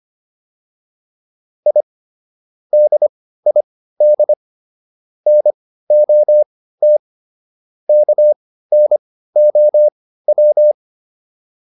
Morse code, heard as a keyed tone, about 25 wpm, spaced as for 15 wpm, 600 Hz. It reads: I DID NOT KNOW